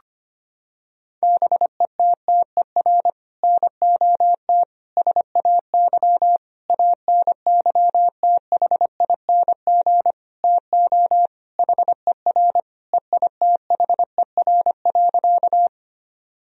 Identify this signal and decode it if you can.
Morse code, heard as a keyed tone, about 25 wpm, 700 Hz. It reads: BETTER NOT SAY ANYTHING TO HER EITHER.
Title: Sentences from Top 1000 Words - Review Part 4 - 25wpm